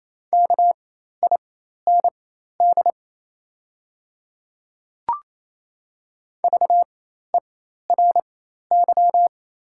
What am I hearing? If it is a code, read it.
Morse code, heard as a keyed tone, about 28 wpm, spaced as for 13 wpm, 690 Hz. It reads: KIND VERY
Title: 4x Character Spacing - Top 200 Words - 28wpm